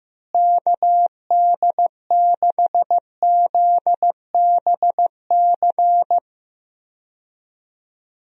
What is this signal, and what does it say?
Morse code, heard as a keyed tone, about 15 wpm, 700 Hz. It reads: KD6ZBC
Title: USA Call Signs - 6 Characters - Repeated 3x - 15wpm